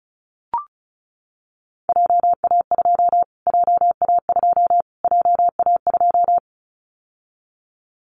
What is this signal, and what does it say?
Morse code, heard as a keyed tone, about 34 wpm, 700 Hz. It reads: JA2 JA2 JA2
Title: Callsign Prefixes for Top 25 DXCC - Repeated 3x - 35wpm